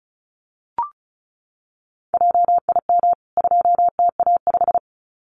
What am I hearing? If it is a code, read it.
Morse code, heard as a keyed tone, about 34 wpm, 700 Hz. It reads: JIM 2TA5
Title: CWT Exchange 35wpm